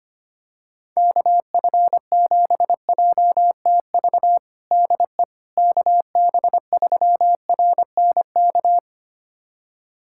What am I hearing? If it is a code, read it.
Morse code, heard as a keyed tone, about 25 wpm, 700 Hz. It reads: KF7JTV DE KB3RNK
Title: USA Callsign DE Callsign 25wpm